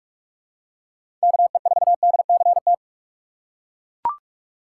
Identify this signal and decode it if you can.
Morse code, heard as a keyed tone, about 42 wpm, 700 Hz. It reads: KE4DCT
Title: USA Call Signs - 2x Character Spacing - 45wpm